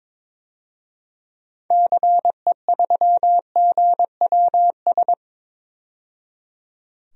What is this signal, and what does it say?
Morse code, heard as a keyed tone, about 21 wpm, 700 Hz. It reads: CE3GWS
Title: DXCC-100 Call Signs - 6 Characters - 22wpm